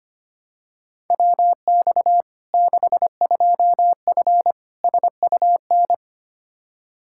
Morse code, heard as a keyed tone, about 25 wpm, 700 Hz. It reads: WX 62F SUN
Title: Practice QSO - Exchange 2 - Info - 25wpm